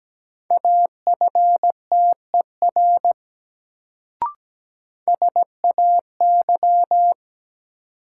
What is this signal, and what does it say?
Morse code, heard as a keyed tone, about 17 wpm, 700 Hz. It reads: AFTER SAY
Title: Sentences from Top 200 Words - New Word Drill - 17wpm